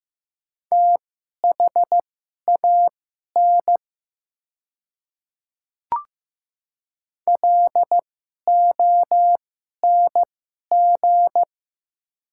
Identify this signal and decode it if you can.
Morse code, heard as a keyed tone, about 15 wpm, spaced as for 11 wpm, 690 Hz. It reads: THAN LONG
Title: 2x Character Spacing - Top 100 Words - 15wpm